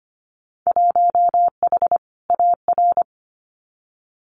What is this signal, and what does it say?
Morse code, heard as a keyed tone, about 25 wpm, 700 Hz. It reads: 1H AR
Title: Winter Field Day Exchanges 25wpm